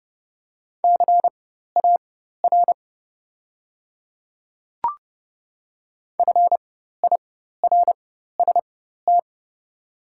Morse code, heard as a keyed tone, about 30 wpm, spaced as for 14 wpm, 700 Hz. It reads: CAR FIRST
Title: 4x Character Spacing - Top 300 Words - 30wpm